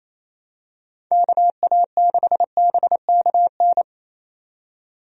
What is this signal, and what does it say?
Morse code, heard as a keyed tone, about 28 wpm, 700 Hz. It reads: KA6BKN
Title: USA Call Signs - 6 Characters - 28wpm